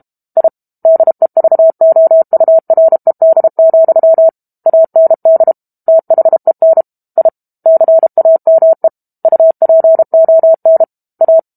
Morse code, heard as a keyed tone, about 32 wpm, 650 Hz. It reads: I DEVOURED, AND THEN I CAME UPON A
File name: Book Two / 06-Chapter 6 - 32 WPM